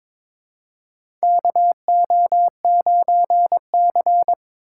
Morse code, heard as a keyed tone, about 22 wpm, 700 Hz. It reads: KO9C
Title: USA Call Signs - 8x Character Spacing - 22wpm